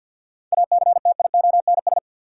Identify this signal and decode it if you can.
Morse code, heard as a keyed tone, about 52 wpm, 700 Hz. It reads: ACTIONS
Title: Sentences from Top 1000 Words - Review Part 6 - New Word Drill - 50wpm